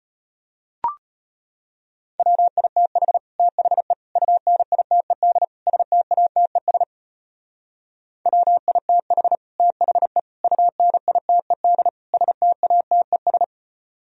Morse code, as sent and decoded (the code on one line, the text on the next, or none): .-- .. - .... / - .... / ..- -. .. - . -.. / ... - .- - . ... / .-- .. - .... / - .... / ..- -. .. - . -.. / ... - .- - . ...
WITH THE UNITED STATES WITH THE UNITED STATES